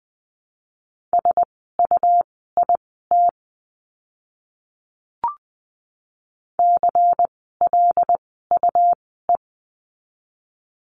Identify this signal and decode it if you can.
Morse code, heard as a keyed tone, about 20 wpm, spaced as for 14 wpm, 700 Hz. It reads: SUIT CLUE
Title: 101 Lesson 17f - Letter u - Words 20wpm